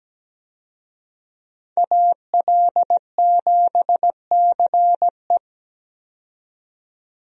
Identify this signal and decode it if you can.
Morse code, heard as a keyed tone, about 17 wpm, 700 Hz. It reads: AL7CE
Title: USA Call Signs - 5 Characters - 17wpm